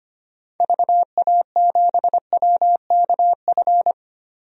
Morse code, heard as a keyed tone, about 25 wpm, 700 Hz. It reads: VA7WKF
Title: DXCC-100 Call Signs - 6 Characters - Repeated 3x - 25wpm